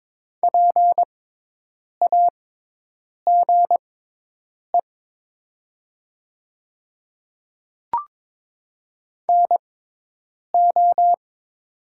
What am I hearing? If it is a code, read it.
Morse code, heard as a keyed tone, about 22 wpm, spaced as for 8 wpm, 700 Hz. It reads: PAGE NO
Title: 6x Character Spacing - Top 500 Words - 22wpm